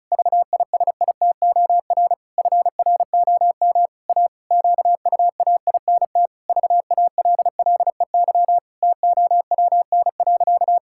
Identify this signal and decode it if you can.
Morse code, heard as a keyed tone, about 36 wpm, 700 Hz. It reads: VISITOR FROM A QUAINT VALLEY TOWN.
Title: Pangram Challenge v2 - 35wpm